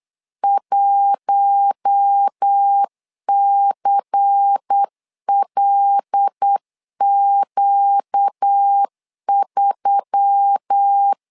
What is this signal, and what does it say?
Morse code, heard as a keyed tone, about 8 wpm, 800 Hz. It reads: 1CLQ3
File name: LEZIONI / 5 LBVC / 5 LBVC 8wpm 2935